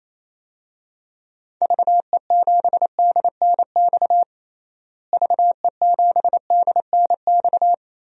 Morse code, heard as a keyed tone, about 28 wpm, 700 Hz. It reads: VE7DNX VE7DNX
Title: DXCC-100 Call Signs - 6 Characters - Repeated 2x - 28wpm